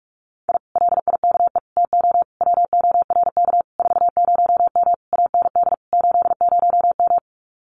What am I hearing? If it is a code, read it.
Morse code, heard as a keyed tone, about 44 wpm, 700 Hz. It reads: I LIKE TO WORK 40M AND 80M
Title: Common QSO Phrases Encore 45wpm